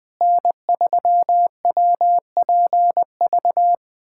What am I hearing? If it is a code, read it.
Morse code, heard as a keyed tone, about 21 wpm, 700 Hz. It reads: N3WPV
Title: USA Callsign DE Callsign - 5 Characters - 20wpm